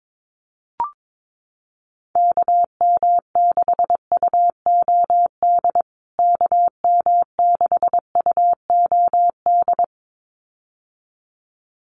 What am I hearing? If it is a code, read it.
Morse code, heard as a keyed tone, about 22 wpm, 700 Hz. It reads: KM6UOD KM6UOD
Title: USA Call Signs Repeated 2x 22wpm